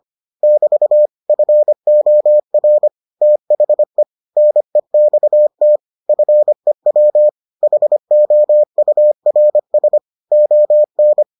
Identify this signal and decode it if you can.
Morse code, heard as a keyed tone, about 25 wpm, 600 Hz. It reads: <BT> FOR THE NEXT FEW HOURS ON